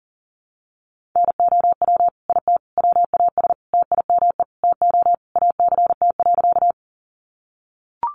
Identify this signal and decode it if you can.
Morse code, heard as a keyed tone, about 40 wpm, 700 Hz. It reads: NOW IT WAS TIME TO ACT.
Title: Sentences from Top 200 Words 40wpm